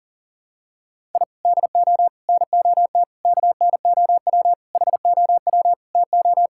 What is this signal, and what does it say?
Morse code, heard as a keyed tone, about 40 wpm, 700 Hz. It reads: I DO NOT KNOW HOW TO
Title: Sentences from Top 1000 Words - Review Part 6 - 40wpm